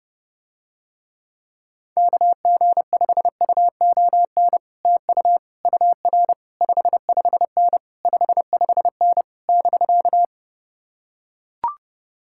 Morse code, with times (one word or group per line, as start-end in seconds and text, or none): 2.0-4.6 KG5UON
4.8-5.4 TU
5.7-6.3 UR
6.6-7.8 55N
8.0-9.2 55N
9.5-10.2 <BK>